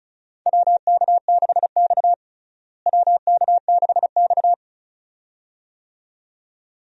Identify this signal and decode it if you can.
Morse code, heard as a keyed tone, about 34 wpm, 700 Hz. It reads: WK6X WK6X
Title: USA Call Signs - 4 Characters - Repeated 2x - 35wpm